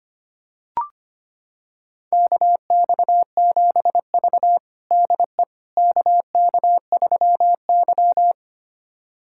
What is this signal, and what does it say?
Morse code, heard as a keyed tone, about 25 wpm, 700 Hz. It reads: KX7V DE KK3Y
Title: USA Callsign DE Callsign - 4 Characters - 25wpm